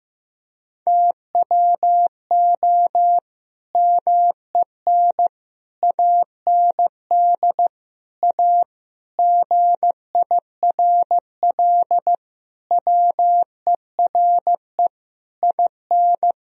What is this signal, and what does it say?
Morse code, heard as a keyed tone, about 15 wpm, 700 Hz. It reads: TWO MEN AND A GIRL WERE IN
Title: Sentences from Top 1000 Words - Review Part 3 - 15wpm